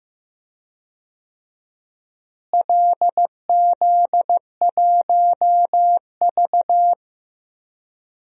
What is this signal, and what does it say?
Morse code, heard as a keyed tone, about 15 wpm, 700 Hz. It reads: LZ1V
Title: DXCC-100 Call Signs - 4 Characters - Repeated 2x - 15wpm